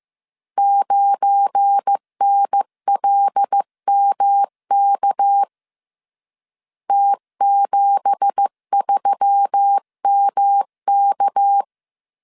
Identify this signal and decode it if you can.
Morse code, heard as a keyed tone, about 15 wpm, 800 Hz. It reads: 9NLMK T73MK
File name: CW / DET / 1 9 1500C S B 14wpm 3302